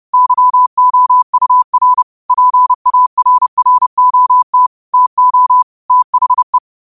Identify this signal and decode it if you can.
Morse code, heard as a keyed tone, about 30 wpm, 1000 Hz. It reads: YOUR PARROT TO THE
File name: qotdfiles / 30wpm